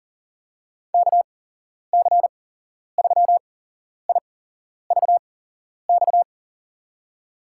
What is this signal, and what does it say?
Morse code, heard as a keyed tone, about 40 wpm, spaced as for 11 wpm, 700 Hz. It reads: KC3IVX
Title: USA Call Signs - 6 Characters - 8x Character Spacing - 40wpm